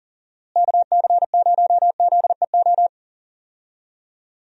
Morse code, heard as a keyed tone, about 37 wpm, 700 Hz. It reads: KC0ZEO
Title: USA Call Signs - 6 Characters - Repeated 2x - 40wpm